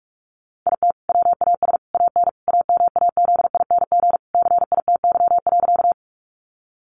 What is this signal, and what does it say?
Morse code, heard as a keyed tone, about 44 wpm, 700 Hz. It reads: IT WAS AN AMAZING CITY.
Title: Sentences from Top 1500 Words 45wpm